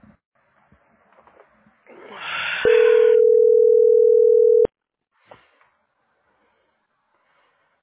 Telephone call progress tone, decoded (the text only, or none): ringback tone